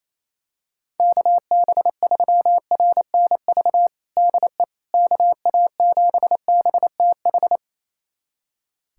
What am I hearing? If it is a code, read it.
Morse code, heard as a keyed tone, about 27 wpm, 700 Hz. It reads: KB3RNV DE KA7BTH
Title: USA Callsign DE Callsign 28wpm